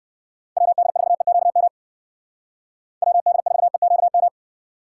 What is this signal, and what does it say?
Morse code, heard as a keyed tone, about 58 wpm, 700 Hz. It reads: WD4EQM WD4EQM